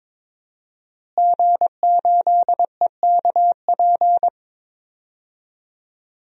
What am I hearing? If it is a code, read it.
Morse code, heard as a keyed tone, about 22 wpm, 700 Hz. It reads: G8EKP
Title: DXCC-100 Call Signs - 5 Characters - Repeated 3x - 22wpm